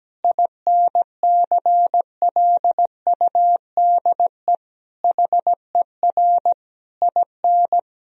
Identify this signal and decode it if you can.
Morse code, heard as a keyed tone, about 17 wpm, 700 Hz. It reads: INCLUDE HER IN
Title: Sentences from Top 1000 Words - Review Part 3 - 17wpm